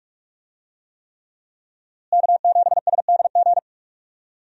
Morse code, heard as a keyed tone, about 44 wpm, 700 Hz. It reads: K7SDG
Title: USA Call Signs - 5 Characters - Repeated 3x - 45wpm